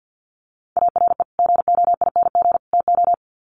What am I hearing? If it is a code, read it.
Morse code, heard as a keyed tone, about 49 wpm, 700 Hz. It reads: ARE GOING TO
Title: Sets of 3 Words 50wpm